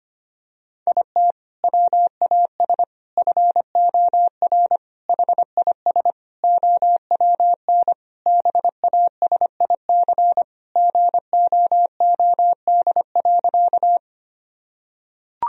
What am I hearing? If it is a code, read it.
Morse code, heard as a keyed tone, about 25 wpm, 700 Hz. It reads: IT WAS FOR HIS OWN BASIC GOOD.
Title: Sentences from Top 1000 Words - Review Part 3 - 25wpm